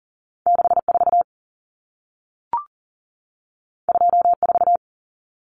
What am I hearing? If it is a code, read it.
Morse code, heard as a keyed tone, about 40 wpm, 700 Hz. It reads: 64 24
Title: Numbers - 2-Digit 40wpm